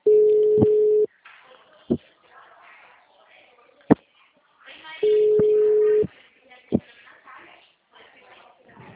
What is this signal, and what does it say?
Ringback tone (Europe, 425 Hz).